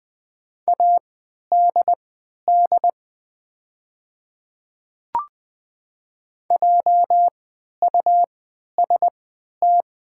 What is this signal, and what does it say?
Morse code, heard as a keyed tone, about 20 wpm, spaced as for 11 wpm, 700 Hz. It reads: ADD JUST
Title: 3x Character Spacing - Top 500 Words - 20wpm